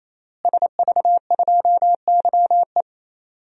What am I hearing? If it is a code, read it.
Morse code, heard as a keyed tone, about 28 wpm, 700 Hz. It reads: SV2YE
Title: DXCC-100 Call Signs - 5 Characters - Repeated 3x - 28wpm